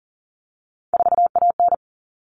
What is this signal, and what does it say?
Morse code, heard as a keyed tone, about 40 wpm, 700 Hz. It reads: 4AN
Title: All Cut Numbers - 3-Digit 40wpm